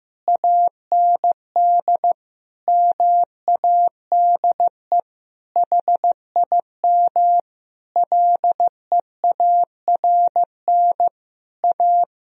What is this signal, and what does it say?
Morse code, heard as a keyed tone, about 15 wpm, 700 Hz. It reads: AND MADE HIM LEARN A
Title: Sentences from Top 1000 Words - Review Part 4 - 15wpm